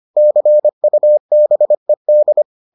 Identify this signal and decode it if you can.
Morse code, heard as a keyed tone, about 25 wpm, 600 Hz. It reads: CUBED